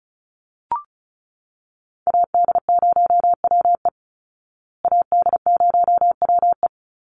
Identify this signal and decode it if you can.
Morse code, heard as a keyed tone, about 34 wpm, 700 Hz. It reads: AD0WE AD0WE
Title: Call Signs Repeated 2x - 35wpm